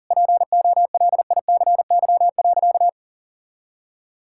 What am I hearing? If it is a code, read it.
Morse code, heard as a keyed tone, about 40 wpm, 700 Hz. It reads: POLICY.
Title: Sentences from Top 1000 Words - Review Part 5 - 40wpm